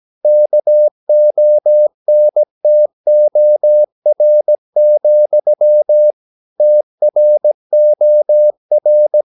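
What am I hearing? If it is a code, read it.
Morse code, heard as a keyed tone, about 17 wpm, 600 Hz. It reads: KONTOR, TROR
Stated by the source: Slow (QRS) Morse practice at 17 wpm, from CW med Gnister.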